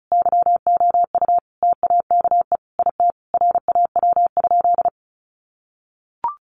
Morse code, heard as a keyed tone, about 34 wpm, 700 Hz. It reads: YOU TAKE IT RAW?
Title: Sentences from Top 1500 Words 35wpm